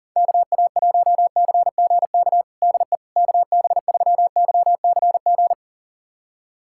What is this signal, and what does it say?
Morse code, heard as a keyed tone, about 40 wpm, 700 Hz. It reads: KA1CGK DE KB3YCG
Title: USA Callsign DE Callsign - 6 Characters - 40wpm